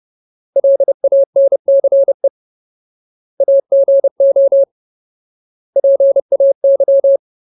Morse code, heard as a keyed tone, about 30 wpm, 550 Hz. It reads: LANCE AGO PAY
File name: Ultimate Word List / 30WPM Ultimate